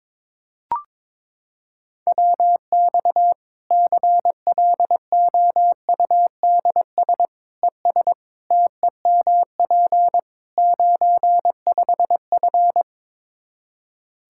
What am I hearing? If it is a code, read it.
Morse code, heard as a keyed tone, about 22 wpm, 700 Hz. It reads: WX CLOUDS ES TEMP 95F
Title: Practice QSO - Exchange 2 - Info - 22wpm